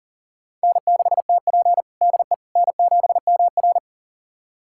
Morse code, heard as a keyed tone, about 40 wpm, 700 Hz. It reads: N6TP DE N7MR
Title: USA Callsign DE Callsign - 4 Characters - 40wpm